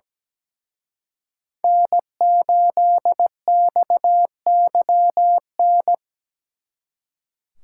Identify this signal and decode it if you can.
Morse code, heard as a keyed tone, about 20 wpm, 700 Hz. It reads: N8XYN
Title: Speed Racing - USA Call Signs 17wpm